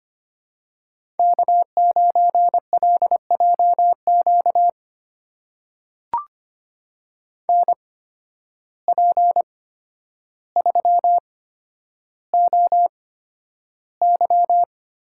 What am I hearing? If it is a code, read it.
Morse code, heard as a keyed tone, about 25 wpm, spaced as for 11 wpm, 700 Hz. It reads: K9LJQ NP3OY
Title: USA Call Signs - 5 Characters - 8x Character Spacing - 25wpm